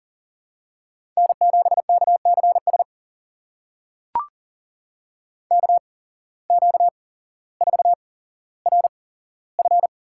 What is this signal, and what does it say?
Morse code, heard as a keyed tone, about 39 wpm, spaced as for 17 wpm, 700 Hz. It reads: N7KCS KQ4RF
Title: USA Call Signs - 5 Characters - 8x Character Spacing - 40wpm